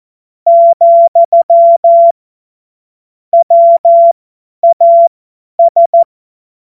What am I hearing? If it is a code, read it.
Morse code, heard as a keyed tone, about 14 wpm, spaced as for 10 wpm, 680 Hz. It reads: , WAS
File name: Morsekurs / Witze / Witz 25. Juli 2023 50 F70